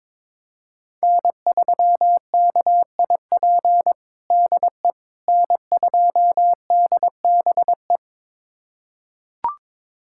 Morse code, heard as a keyed tone, about 22 wpm, 700 Hz. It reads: N3KIP DE N2DBE
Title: USA Callsign DE Callsign - 5 Characters - 22wpm